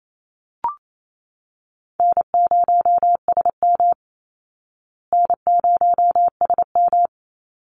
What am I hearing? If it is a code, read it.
Morse code, heard as a keyed tone, about 28 wpm, 700 Hz. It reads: N0SM N0SM
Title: Call Signs Repeated 2x - 28wpm